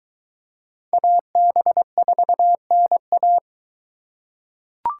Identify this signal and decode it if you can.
Morse code, heard as a keyed tone, about 22 wpm, 700 Hz. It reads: AB4NA